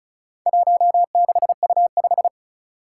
Morse code, heard as a keyed tone, about 36 wpm, 700 Hz. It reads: 16U5
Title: Four Challenging Letter-Number Combinations 35wpm